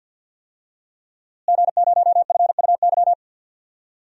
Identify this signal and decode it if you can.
Morse code, heard as a keyed tone, about 53 wpm, 700 Hz. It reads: G0FUY
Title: DXCC-100 Call Signs - 5 Characters - Repeated 2x - 50wpm